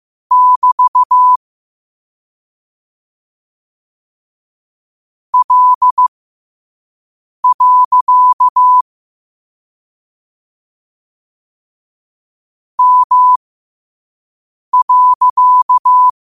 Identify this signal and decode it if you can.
Morse code, heard as a keyed tone, about 15 wpm, spaced as for 5 wpm, 1000 Hz. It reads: <BT> L. M.